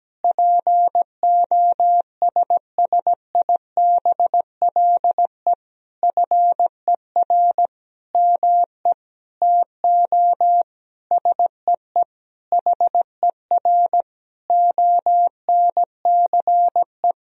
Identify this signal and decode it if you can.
Morse code, heard as a keyed tone, about 17 wpm, 700 Hz. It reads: POSSIBLE FER ME TO SEE HER ONCE
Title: Sentences from Top 500 Words and Abbreviations 17wpm